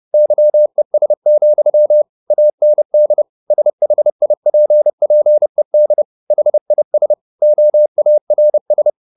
Morse code, heard as a keyed tone, about 30 wpm, 600 Hz. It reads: YES, AND SHIPPED HIS OARS
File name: the old man and the sea / the old man and the sea 1 30wpm